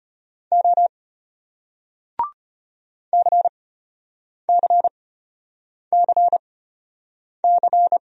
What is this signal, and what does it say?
Morse code, heard as a keyed tone, about 34 wpm, spaced as for 8 wpm, 700 Hz. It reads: O CCCC